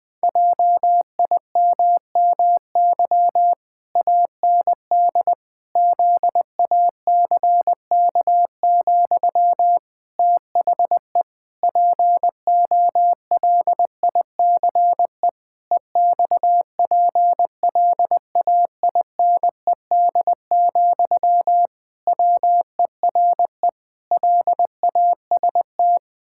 Morse code, heard as a keyed tone, about 21 wpm, 700 Hz. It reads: JIMMY AND ZACK, THE POLICE EXPLAINED, WERE LAST
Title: Pangram Challenge v2 - 20wpm